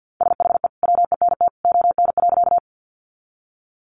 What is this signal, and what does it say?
Morse code, heard as a keyed tone, about 50 wpm, 700 Hz. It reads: SHE WENT ON.
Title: Sentences from Top 200 Words 50wpm